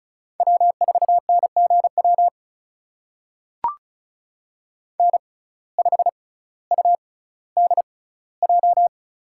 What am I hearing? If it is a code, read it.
Morse code, heard as a keyed tone, about 36 wpm, spaced as for 18 wpm, 700 Hz. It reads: W4NGW N5UDJ
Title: USA Call Signs - 5 Characters - 6x Character Spacing - 35wpm